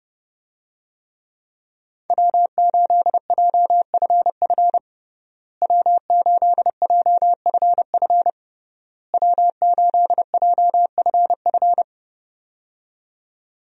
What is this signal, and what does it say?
Morse code, heard as a keyed tone, about 32 wpm, 700 Hz. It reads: W8JFF W8JFF W8JFF
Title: USA Call Signs - 5 Characters - Repeated 3x - 30wpm